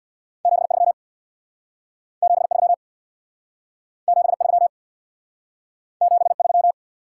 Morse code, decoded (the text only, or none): 73 73 73 73